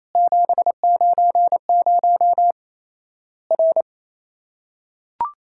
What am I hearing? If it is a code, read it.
Morse code, heard as a keyed tone, about 28 wpm, 700 Hz. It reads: 790 R
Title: Worked All Europe DX Contest - QTC Exchange - 28wpm